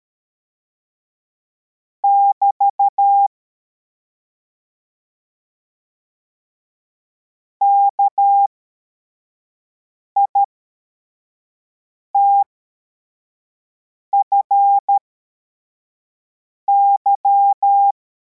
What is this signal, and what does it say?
Morse code, heard as a keyed tone, about 13 wpm, spaced as for 4 wpm, 800 Hz. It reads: <BT> KITFY